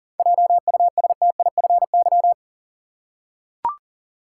Morse code, heard as a keyed tone, about 40 wpm, 700 Hz. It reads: JUSTIFY
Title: Seven Letter Words - 40wpm